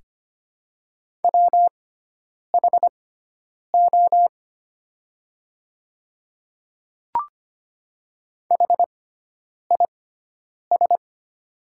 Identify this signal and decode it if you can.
Morse code, heard as a keyed tone, about 25 wpm, spaced as for 9 wpm, 700 Hz. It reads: WHO HIS
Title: 6x Character Spacing - Top 100 Words - 25wpm